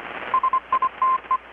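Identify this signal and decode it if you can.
Morse code, heard as a keyed tone, about 25 wpm, 1040 Hz, with radio noise. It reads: SITE
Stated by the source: Morse from KPH's Night of Nights XX maritime radiotelegraphy event, received in Chiapas, Mexico about 3,800 km away on a Kenwood R-600 with a grounded 10 m long-wire antenna.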